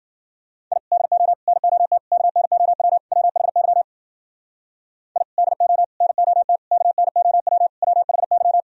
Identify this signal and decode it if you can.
Morse code, heard as a keyed tone, about 52 wpm, 700 Hz. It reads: I DO NOT KNOW WHY I DO NOT KNOW WHY